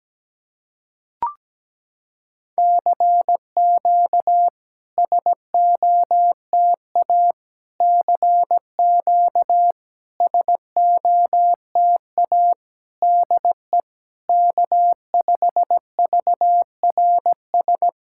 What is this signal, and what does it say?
Morse code, heard as a keyed tone, about 17 wpm, 700 Hz. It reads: CQ SOTA CQ SOTA DE K5VRS
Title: SOTA QSO Practice 17wpm